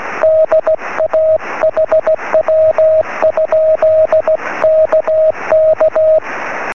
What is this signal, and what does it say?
Morse code, heard as a keyed tone, about 16 wpm, 630 Hz, with radio noise. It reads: DA HW? KK
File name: sm7lcb / ua1ali 23cm